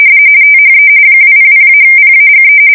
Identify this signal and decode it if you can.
RTTY, about 45.45 baud. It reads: THE 200-PAGE
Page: 170 Shift 2125/2295